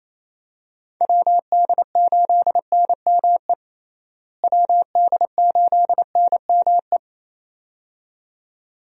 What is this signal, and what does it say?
Morse code, heard as a keyed tone, about 27 wpm, 700 Hz. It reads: WD8NME WD8NME
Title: USA Call Signs - 6 Characters - Repeated 2x - 28wpm